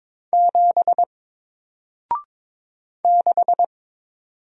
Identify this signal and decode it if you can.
Morse code, heard as a keyed tone, about 22 wpm, 700 Hz. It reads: 7 6
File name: Single-Letter-Number-22wpm